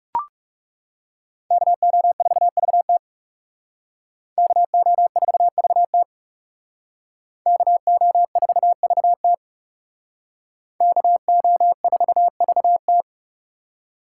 Morse code, heard as a keyed tone, about 37 wpm, 700 Hz. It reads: KO4VT KO4VT KO4VT KO4VT
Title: Speed Racing - USA 2x2 Call Signs 30wpm